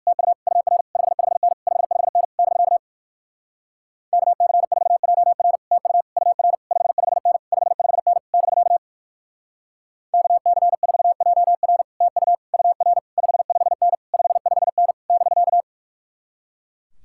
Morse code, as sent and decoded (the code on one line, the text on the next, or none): - ..- / ..- .-. / ..... ..... -. / ..... ..... -. / -...-.- / -.- -.-. ....- .--- .-. / - ..- / ..- .-. / ..... ..... -. / ..... ..... -. / -...-.- / -.- -.-. ....- .--- .-. / - ..- / ..- .-. / ..... ..... -. / ..... ..... -. / -...-.-
TU UR 55N 55N <BK> KC4JR TU UR 55N 55N <BK> KC4JR TU UR 55N 55N <BK>